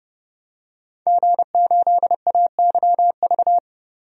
Morse code, heard as a keyed tone, about 30 wpm, 700 Hz. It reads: G8AYV
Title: DXCC-100 Call Signs - 5 Characters - 30wpm